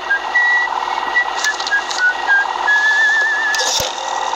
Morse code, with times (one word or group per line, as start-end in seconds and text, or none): none